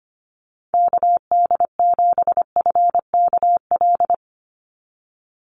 Morse code, heard as a keyed tone, about 25 wpm, 700 Hz. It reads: KD7FKL
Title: USA Call Signs 25wpm